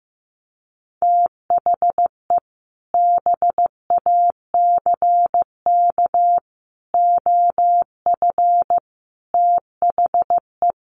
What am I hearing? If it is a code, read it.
Morse code, heard as a keyed tone, about 14 wpm, 700 Hz. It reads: THE BACK OF THE
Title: Sets of 5 Words Encore 15wpm